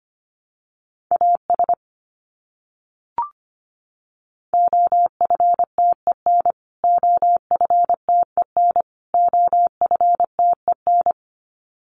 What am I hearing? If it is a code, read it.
Morse code, heard as a keyed tone, about 25 wpm, 700 Hz. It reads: AS OFTEN OFTEN OFTEN
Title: Top 500 Words Repeated 25wpm